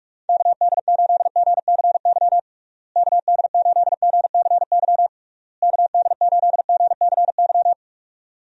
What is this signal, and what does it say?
Morse code, heard as a keyed tone, about 41 wpm, 700 Hz. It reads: KD8GCY KD8GCY KD8GCY
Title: USA Call Signs - 6 Characters - Repeated 3x - 45wpm